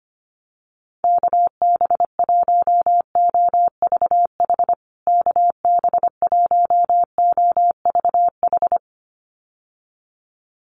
Morse code, heard as a keyed tone, about 25 wpm, 700 Hz. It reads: KB1OVH KB1OVH
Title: USA Call Signs Repeated 2x 25wpm